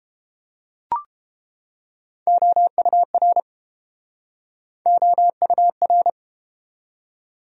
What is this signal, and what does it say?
Morse code, heard as a keyed tone, about 31 wpm, 700 Hz. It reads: OUR OUR